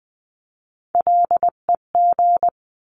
Morse code, heard as a keyed tone, about 20 wpm, 700 Hz. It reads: LEG
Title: 259 Lesson 42d - Words 18-20wpm